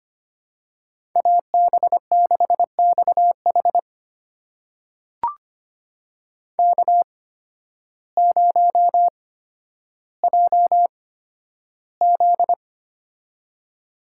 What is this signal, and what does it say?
Morse code, heard as a keyed tone, about 25 wpm, spaced as for 12 wpm, 700 Hz. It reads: AB6XH K0JZ
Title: USA Call Signs - 8x Character Spacing - 25wpm